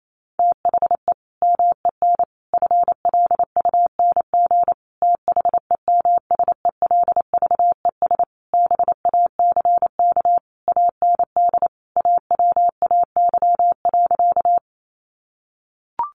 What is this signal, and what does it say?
Morse code, heard as a keyed tone, about 28 wpm, 700 Hz. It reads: THE MEN FLUNG THEMSELVES BACK AND AWAY.
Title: A Sound Of Thunder 28wpm